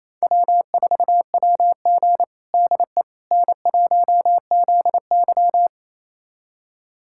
Morse code, heard as a keyed tone, about 28 wpm, 700 Hz. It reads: W4WG DE N1ZY
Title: USA Callsign DE Callsign - 4 Characters - 28wpm